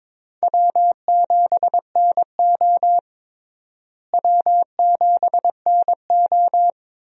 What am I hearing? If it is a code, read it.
Morse code, heard as a keyed tone, about 21 wpm, 700 Hz. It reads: W7NO W7NO
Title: USA Call Signs - 4 Characters - Repeated 2x - 22wpm